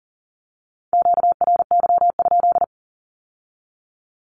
Morse code, heard as a keyed tone, about 40 wpm, 700 Hz. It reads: QRY?
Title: Q-Codes with Spoken Meaning - 40wpm